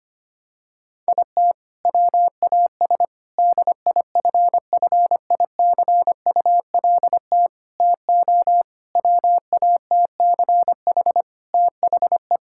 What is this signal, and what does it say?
Morse code, heard as a keyed tone, about 25 wpm, 700 Hz. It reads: IT WAS DIFFICULT TO WATCH THE
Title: Sentences from Top 1000 Words - Review Part 5 - 25wpm